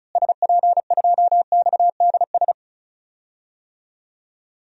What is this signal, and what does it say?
Morse code, heard as a keyed tone, about 34 wpm, 700 Hz. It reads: SP2XDS
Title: DXCC-100 Call Signs - 6 Characters - Repeated 2x - 35wpm